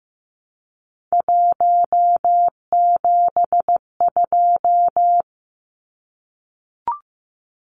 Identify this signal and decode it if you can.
Morse code, heard as a keyed tone, about 15 wpm, 700 Hz. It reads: 172
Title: Numbers - 3-Digit 15wpm